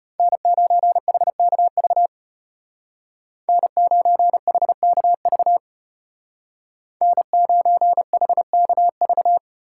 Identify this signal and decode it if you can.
Morse code, heard as a keyed tone, about 33 wpm, 700 Hz. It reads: N9HKV N9HKV N9HKV